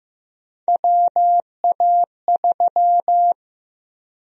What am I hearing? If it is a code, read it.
Morse code, heard as a keyed tone, about 15 wpm, 700 Hz. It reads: WA3
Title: USA Call Sign Prefixes - Repeated 2x - 15wpm